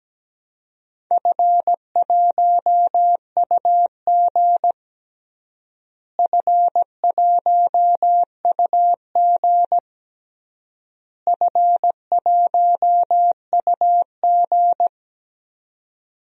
Morse code, heard as a keyed tone, about 17 wpm, 700 Hz. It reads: F1UG F1UG F1UG
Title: DXCC-100 Call Signs - 4 Characters - Repeated 3x - 17wpm